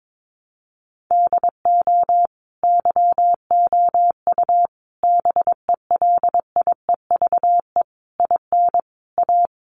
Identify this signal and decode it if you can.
Morse code, heard as a keyed tone, about 22 wpm, 700 Hz. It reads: DO YOU BELIEVE IN A
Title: Sentences from Top 1000 Words 22wpm